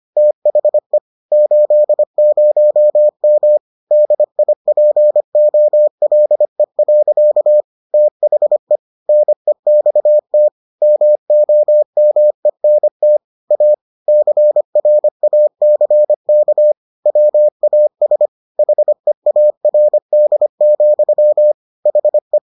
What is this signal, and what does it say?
Morse code, heard as a keyed tone, about 25 wpm, 600 Hz. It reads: THE 80M DIPOLE. THE NEXT MOMENT A CRACK WAS HEARD, HE